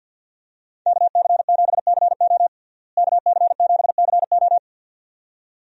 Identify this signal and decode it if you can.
Morse code, heard as a keyed tone, about 50 wpm, 700 Hz. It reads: KC7CO KC7CO
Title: USA Call Signs - 5 Characters - Repeated 2x - 50wpm